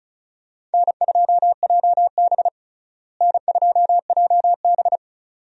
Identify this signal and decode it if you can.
Morse code, heard as a keyed tone, about 34 wpm, 700 Hz. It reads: N2JB N2JB
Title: USA Call Signs - 4 Characters - Repeated 2x - 35wpm